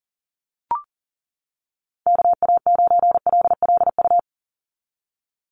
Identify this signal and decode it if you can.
Morse code, heard as a keyed tone, about 40 wpm, 700 Hz. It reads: KA9LLU
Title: USA Call Signs 40wpm